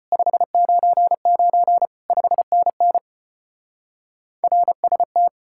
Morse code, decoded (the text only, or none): 599 5NN RST